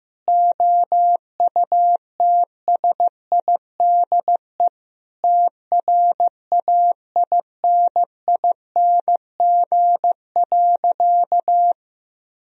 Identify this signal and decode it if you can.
Morse code, heard as a keyed tone, about 16 wpm, 700 Hz. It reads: OUTSIDE TRAINING.
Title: Sentences from Top 1000 Words - Review Part 4 - 15wpm